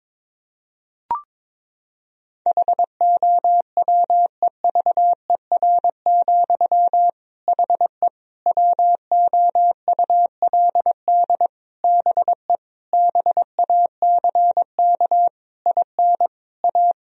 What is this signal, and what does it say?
Morse code, heard as a keyed tone, about 22 wpm, 700 Hz. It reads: HOWEVER, HE WOULD BE BACK IN A
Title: Sentences from Top 1000 Words - Review Part 2 - 22wpm